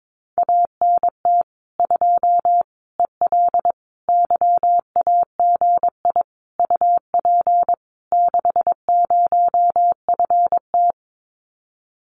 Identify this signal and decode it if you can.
Morse code, heard as a keyed tone, about 22 wpm, 700 Hz. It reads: ANT 2 EL YAGI UP 60FT
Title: Common QSO Phrases Encore 22wpm